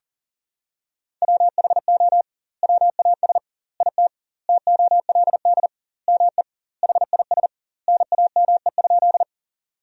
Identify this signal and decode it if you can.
Morse code, heard as a keyed tone, about 40 wpm, 700 Hz. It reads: WHO WAS IT TOLD ME HIS NAME?
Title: 2x Word-Spacing - Sentences from Top 400 Words - 40wpm